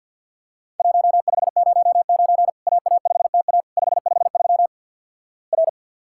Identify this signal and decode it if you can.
Morse code, heard as a keyed tone, about 49 wpm, 700 Hz. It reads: 1509 AA5TA 553 R
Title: Worked All Europe DX Contest - QTC Exchange with Repeats - 50wpm